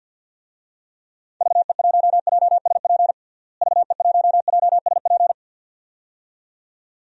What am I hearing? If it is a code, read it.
Morse code, heard as a keyed tone, about 52 wpm, 700 Hz. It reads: VE1JSP VE1JSP
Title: DXCC-100 Call Signs - 6 Characters - Repeated 2x - 50wpm